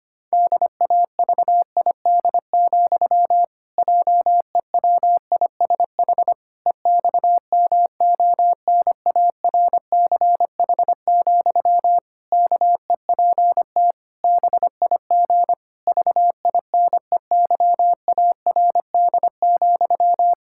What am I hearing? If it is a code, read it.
Morse code, heard as a keyed tone, about 25 wpm, 700 Hz. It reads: DAVID, JEWISH EXMONARCH, KEPT BIG VINEYARD,
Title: Pangram Challenge v2 - 25wpm